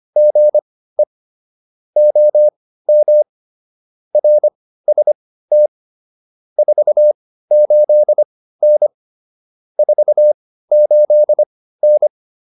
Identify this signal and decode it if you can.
Morse code, heard as a keyed tone, about 25 wpm, spaced as for 15 wpm, 600 Hz. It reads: GE OM RST 48N 48N